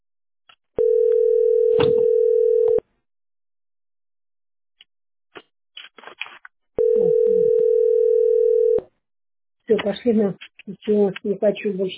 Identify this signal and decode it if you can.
Ringback tone (North America), 440 + 480 Hz, with speech.
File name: aa84b048-138b-4bff-b742